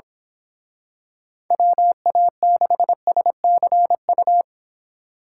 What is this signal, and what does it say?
Morse code, heard as a keyed tone, about 27 wpm, 700 Hz. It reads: WA6SCU
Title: Speed Racing - USA 2x3 Call Signs 22wpm